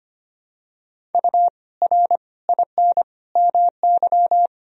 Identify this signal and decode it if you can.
Morse code, heard as a keyed tone, about 25 wpm, 700 Hz. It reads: U R IN MY
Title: Sentences from Top 500 Words and Abbreviations 25wpm